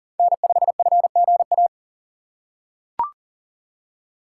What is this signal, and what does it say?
Morse code, heard as a keyed tone, about 37 wpm, 700 Hz. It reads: N5FGA
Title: USA Call Signs - 5 Characters - 6x Character Spacing - 40wpm